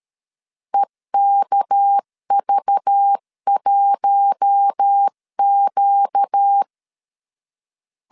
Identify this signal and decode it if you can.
Morse code, heard as a keyed tone, about 13 wpm, 800 Hz. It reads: EKV1Q
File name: LEZIONI / 7 GF9Z / 7 GF 12wpm 2122